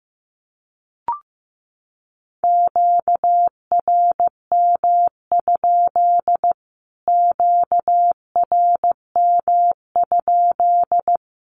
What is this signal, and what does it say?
Morse code, heard as a keyed tone, about 15 wpm, 700 Hz. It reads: QRM? QRM?
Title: Q-Codes Repeated 3x 15wpm